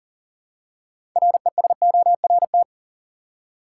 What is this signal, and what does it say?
Morse code, heard as a keyed tone, about 38 wpm, 700 Hz. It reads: RESORT
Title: Six Letter Words - 40wpm